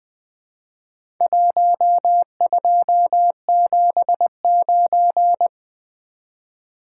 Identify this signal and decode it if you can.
Morse code, heard as a keyed tone, about 20 wpm, 700 Hz. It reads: 1279
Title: Four Letter-Number Combinations - 20wpm